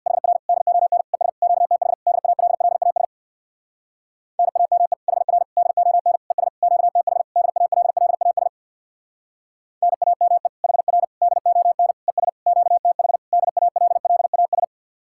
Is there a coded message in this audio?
Yes: Morse code from 0.1 to 14.6 s, heard as a keyed tone, about 66 wpm, 700 Hz. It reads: HR DON ES QTH DALLAS NAME HR DON ES QTH DALLAS NAME HR DON ES QTH DALLAS